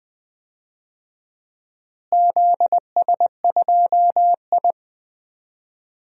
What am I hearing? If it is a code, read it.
Morse code, heard as a keyed tone, about 20 wpm, 700 Hz. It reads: ZS2I
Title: DXCC-100 Call Signs - 4 Characters - Repeated 3x - 20wpm